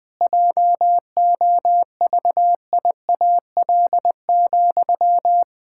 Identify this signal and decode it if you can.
Morse code, heard as a keyed tone, about 21 wpm, 700 Hz. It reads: JOVIAL,
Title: Pangram Challenge v2 - 20wpm